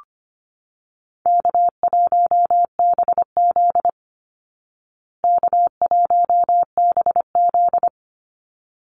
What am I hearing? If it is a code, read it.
Morse code, heard as a keyed tone, about 25 wpm, 700 Hz. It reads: K1BZ K1BZ
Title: Easy Call Signs Repeated 2x - 25wpm